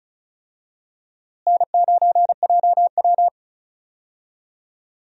Morse code, heard as a keyed tone, about 34 wpm, 700 Hz. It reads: N9JW
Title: USA Call Signs - 4 Characters - Repeated 3x - 35wpm